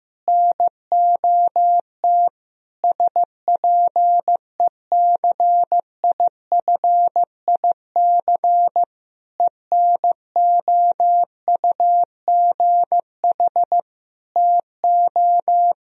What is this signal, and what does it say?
Morse code, heard as a keyed tone, about 15 wpm, 700 Hz. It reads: NOT SPECIFIC ENOUGH TO
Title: Sentences from Top 1000 Words - Review Part 3 - 15wpm